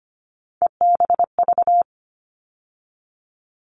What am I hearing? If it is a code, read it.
Morse code, heard as a keyed tone, about 25 wpm, 700 Hz. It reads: EBV
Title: All Cut Numbers - 3-Digit 25wpm